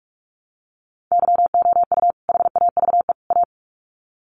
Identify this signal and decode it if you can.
Morse code, heard as a keyed tone, about 44 wpm, 700 Hz. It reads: YOU HAVE A
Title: Sets of 3 Words 45wpm